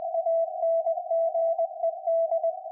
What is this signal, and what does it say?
Morse code, heard as a keyed tone, about 20 wpm, 660 Hz. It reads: ANGED
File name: IC-7100